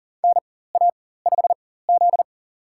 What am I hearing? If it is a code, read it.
Morse code, heard as a keyed tone, about 40 wpm, spaced as for 19 wpm, 700 Hz. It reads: NA5Z
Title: USA Call Signs - 4 Characters - 4x Character Spacing - 40wpm